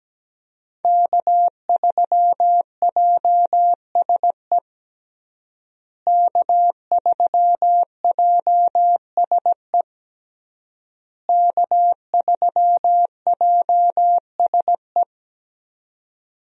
Morse code, heard as a keyed tone, about 17 wpm, 700 Hz. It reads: K3JSE K3JSE K3JSE
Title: USA Call Signs - 5 Characters - Repeated 3x - 17wpm